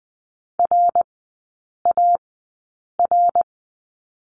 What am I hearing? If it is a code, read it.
Morse code, heard as a keyed tone, about 20 wpm, spaced as for 8 wpm, 700 Hz. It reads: RAR
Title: 053 Lesson 10e - Letter r - Three Characters 20wpm